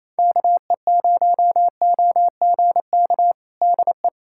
Morse code, heard as a keyed tone, about 28 wpm, 700 Hz. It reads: KE0OGK DE
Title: USA Callsign DE Callsign - 6 Characters - 28wpm